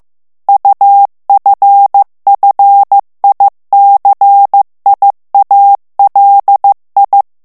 Morse code, heard as a keyed tone, about 15 wpm, 800 Hz. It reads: UFFICIALI
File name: morse mp3 / ITALIEN texte clair 2 vitesse 14 mots minute